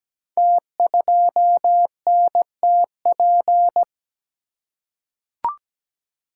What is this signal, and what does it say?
Morse code, heard as a keyed tone, about 17 wpm, 700 Hz. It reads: T2NTP
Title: Code Groups - 5 Letter:Numbers - 17wpm